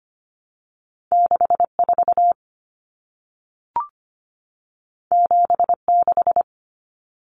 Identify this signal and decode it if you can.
Morse code, heard as a keyed tone, about 25 wpm, 700 Hz. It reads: 64 76
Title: Numbers - 2-Digit 25wpm